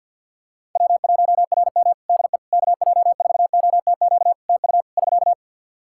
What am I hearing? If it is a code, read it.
Morse code, heard as a keyed tone, about 52 wpm, 700 Hz. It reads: W1RM DE KJ4OTQ TU <SK>
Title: Practice QSO - Exchange 4 - Ending - 50wpm